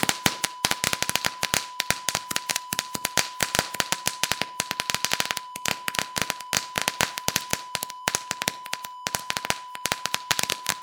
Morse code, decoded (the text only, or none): none